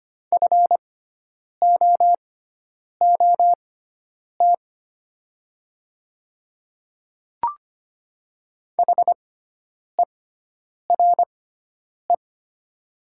Morse code, heard as a keyed tone, about 25 wpm, spaced as for 9 wpm, 700 Hz. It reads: FOOT HERE
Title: 6x Character Spacing - Top 500 Words - 25wpm